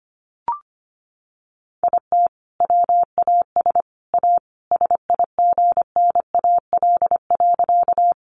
Morse code, heard as a keyed tone, about 25 wpm, 700 Hz. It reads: IT WAS A SIGNAL.
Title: Sentences from Top 1400 Words 25wpm